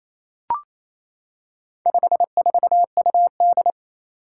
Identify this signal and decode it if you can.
Morse code, heard as a keyed tone, about 29 wpm, 700 Hz. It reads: 54UD
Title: Four Challenging Letter-Number Combinations 28wpm